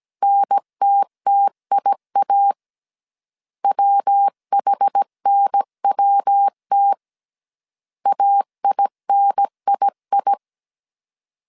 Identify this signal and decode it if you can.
Morse code, heard as a keyed tone, about 17 wpm, 800 Hz. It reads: NTTIA WHNWT AINII
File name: CW / DET / 1 3 1500C S A 16wpm 2141